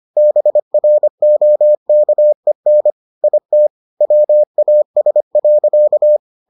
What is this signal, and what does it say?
Morse code, heard as a keyed tone, about 25 wpm, 600 Hz. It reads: BROKEN IT WAS.